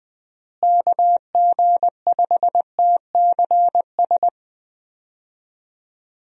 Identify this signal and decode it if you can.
Morse code, heard as a keyed tone, about 20 wpm, 700 Hz. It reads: KG5TCS
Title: USA Call Signs - 6 Characters - Repeated 2x - 20wpm